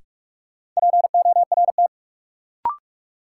Morse code, heard as a keyed tone, about 46 wpm, 700 Hz. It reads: PORT
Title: Sentences from Top 1000 Words - Review Part 1 - New Word Drill - 45wpm